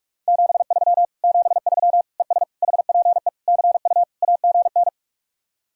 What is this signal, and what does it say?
Morse code, heard as a keyed tone, about 58 wpm, 700 Hz. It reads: 73 73 ES HPE CU AGN